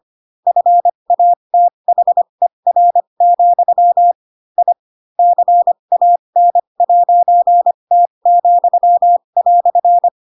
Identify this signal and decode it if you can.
Morse code, heard as a keyed tone, about 25 wpm, spaced as for 22 wpm, 700 Hz. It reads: FATHER, I CAN'T,"
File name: Pinocchio / Pinocchio ch7 18 25 ik1hge